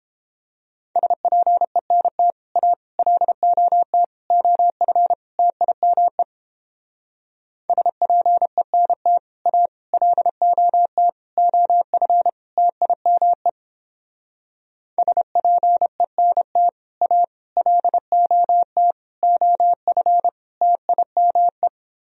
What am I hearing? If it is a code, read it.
Morse code, heard as a keyed tone, about 29 wpm, 700 Hz. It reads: SPENT A LOT OF TIME SPENT A LOT OF TIME SPENT A LOT OF TIME